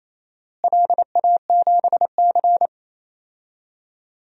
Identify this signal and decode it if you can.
Morse code, heard as a keyed tone, about 28 wpm, 700 Hz. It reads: LA7C
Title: DXCC-100 Call Signs - 4 Characters - Repeated 3x - 28wpm